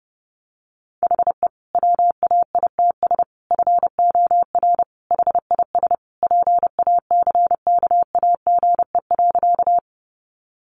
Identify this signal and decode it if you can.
Morse code, heard as a keyed tone, about 28 wpm, 700 Hz. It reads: HE WAITS FOR HIS PACKAGE.
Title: Additional Easy Phrases 30wpm